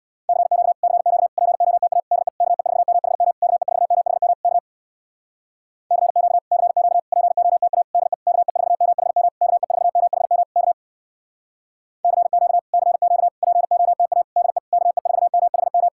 Morse code, heard as a keyed tone, about 64 wpm, 700 Hz. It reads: CQ CQ POTA DE KE4MHM KE4MHM K CQ CQ POTA DE KE4MHM KE4MHM K CQ CQ POTA DE KE4MHM